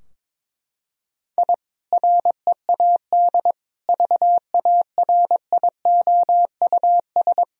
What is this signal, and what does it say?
Morse code, heard as a keyed tone, about 22 wpm, 700 Hz. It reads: I READ VARIOUS
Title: Sentences from Top 1000 Words - Review Part 4 - 22wpm